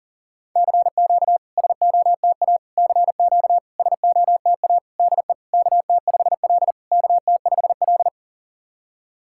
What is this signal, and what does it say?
Morse code, heard as a keyed tone, about 40 wpm, 690 Hz. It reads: CQ SOTA CQ SOTA DE KT5L KT5L
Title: SOTA QSO Practice 40wpm